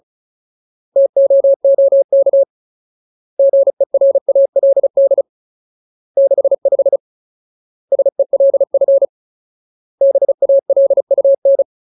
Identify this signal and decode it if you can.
Morse code, heard as a keyed tone, about 34 wpm, 550 Hz. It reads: TOOK GERALD 65 SELF BALUN
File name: Ultimate Word List / 35WPM Ultimate